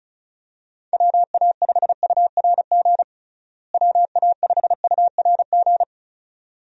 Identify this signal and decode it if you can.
Morse code, heard as a keyed tone, about 36 wpm, 700 Hz. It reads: WA5URG WA5URG
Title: USA Call Signs - 6 Characters - Repeated 2x - 35wpm